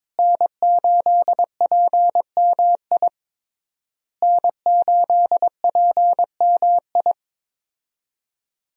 Morse code, heard as a keyed tone, about 22 wpm, 700 Hz. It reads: N8PMI N8PMI
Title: USA Call Signs - 5 Characters - Repeated 3x - 22wpm